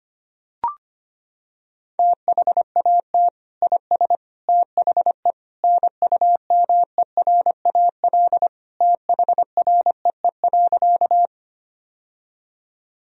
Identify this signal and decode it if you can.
Morse code, heard as a keyed tone, about 25 wpm, 700 Hz. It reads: THAT IS THE NUMERAL THREE.
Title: Sentences from Top 1000 Words - Review Part 2 - 25wpm